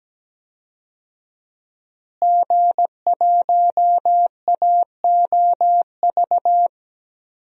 Morse code, heard as a keyed tone, about 17 wpm, 700 Hz. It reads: G1AOV
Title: DXCC-100 Call Signs - 5 Characters - Repeated 3x - 17wpm